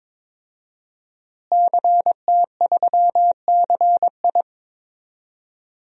Morse code, heard as a keyed tone, about 21 wpm, 700 Hz. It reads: CT3CI
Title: DXCC-100 Call Signs - 5 Characters - Repeated 3x - 22wpm